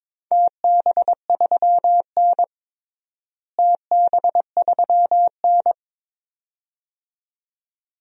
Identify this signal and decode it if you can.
Morse code, heard as a keyed tone, about 22 wpm, 700 Hz. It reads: TB3N TB3N
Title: DXCC-100 Call Signs - 4 Characters - Repeated 2x - 22wpm